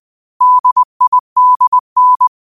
Morse code, heard as a keyed tone, about 20 wpm, 1000 Hz. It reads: DIDN